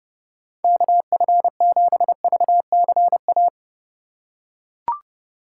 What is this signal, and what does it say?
Morse code, heard as a keyed tone, about 30 wpm, 700 Hz. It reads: KF7VCA
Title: USA Call Signs - 6 Characters - 2x Character Spacing - 30wpm